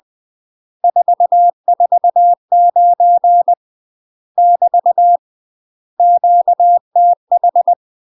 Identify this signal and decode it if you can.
Morse code, heard as a keyed tone, about 20 wpm, 700 Hz. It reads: 449 <BT> QTH
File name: UK MASTER QSOs TRAINING / 20WPM / UK MASTER QSOs 0041